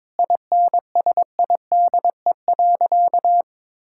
Morse code, heard as a keyed tone, about 22 wpm, 700 Hz. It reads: INSIDE.
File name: practice / 22wpm / Sentences with 2-Word Phrases - 22wpm